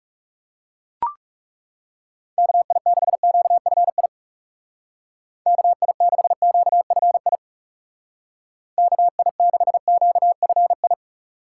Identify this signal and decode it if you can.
Morse code, heard as a keyed tone, about 35 wpm, 700 Hz. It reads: KI6QFI KI6QFI KI6QFI
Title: Speed Racing - USA 2x3 Call Signs 30wpm